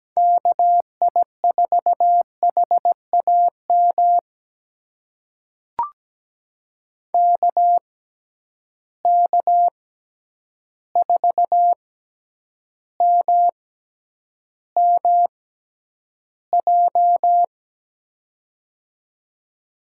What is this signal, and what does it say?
Morse code, heard as a keyed tone, about 17 wpm, spaced as for 9 wpm, 700 Hz. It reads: KI4HAM KK4MMJ
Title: USA Call Signs - 6 Characters - 6x Character Spacing - 17wpm